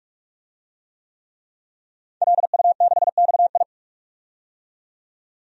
Morse code, heard as a keyed tone, about 44 wpm, 690 Hz. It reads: LU6XI
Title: DXCC-100 Call Signs - 5 Characters - 45wpm